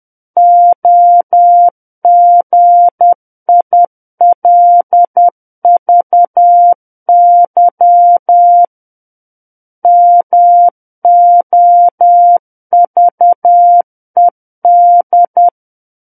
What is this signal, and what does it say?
Morse code, heard as a keyed tone, about 10 wpm, 700 Hz. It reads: OGILVY MOVED
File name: Book One / 01-Chapter 1 - 10 WPM